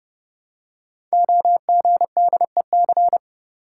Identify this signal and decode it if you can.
Morse code, heard as a keyed tone, about 28 wpm, 700 Hz. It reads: OGDEC
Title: Code Groups - 5 Letters - 30wpm